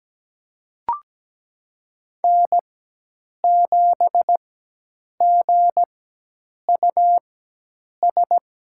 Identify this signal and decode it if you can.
Morse code, heard as a keyed tone, about 17 wpm, spaced as for 8 wpm, 700 Hz. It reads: N7GUS
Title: USA Call Signs - 5 Characters - 4x Character Spacing - 17wpm